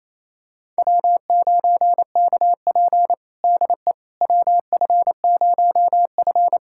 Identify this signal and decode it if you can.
Morse code, heard as a keyed tone, about 28 wpm, 700 Hz. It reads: W9KP DE WF0F
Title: USA Callsign DE Callsign - 4 Characters - 28wpm